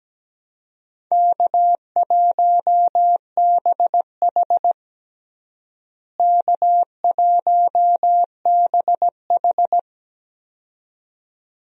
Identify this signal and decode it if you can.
Morse code, heard as a keyed tone, about 17 wpm, 700 Hz. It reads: K1BH K1BH
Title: USA Call Signs - 4 Characters - Repeated 2x - 17wpm